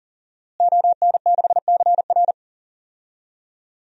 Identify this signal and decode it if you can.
Morse code, heard as a keyed tone, about 40 wpm, 700 Hz. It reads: ON6CR
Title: DXCC-100 Call Signs - 5 Characters - Repeated 2x - 40wpm